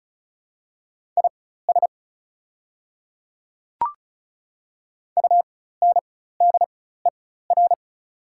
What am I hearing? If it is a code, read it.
Morse code, heard as a keyed tone, about 34 wpm, spaced as for 16 wpm, 690 Hz. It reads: IS UNDER
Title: 4x Character Spacing - Top 200 Words - 35wpm